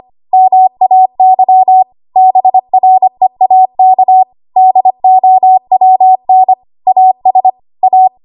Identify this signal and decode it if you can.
Morse code, heard as a keyed tone, about 25 wpm, 750 Hz. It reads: MAY BREAK DOWN AS A
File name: Archive / 25 WPM / 25WPM